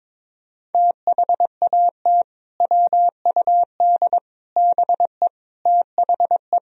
Morse code, heard as a keyed tone, about 22 wpm, 700 Hz. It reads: THAT WUD BE THE
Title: Sentences from Top 400 Words and Abbreviations 22wpm